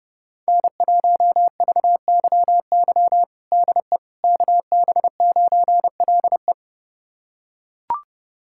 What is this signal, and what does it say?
Morse code, heard as a keyed tone, about 30 wpm, 700 Hz. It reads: N1VYY DE KB9LE
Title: USA Callsign DE Callsign - 5 Characters - 30wpm